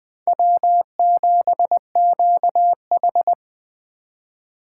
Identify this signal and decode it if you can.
Morse code, heard as a keyed tone, about 20 wpm, 700 Hz. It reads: W7QH
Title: USA Call Signs - 3x Character Spacing - 20wpm